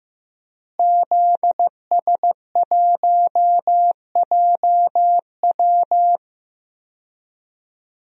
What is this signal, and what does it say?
Morse code, heard as a keyed tone, about 15 wpm, 700 Hz. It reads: ZS1JW
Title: DXCC-100 Call Signs - 5 Characters - Repeated 3x - 15wpm